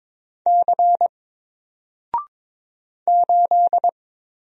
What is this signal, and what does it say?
Morse code, heard as a keyed tone, about 22 wpm, 700 Hz. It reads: C 8
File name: Single-Letter-Number-22wpm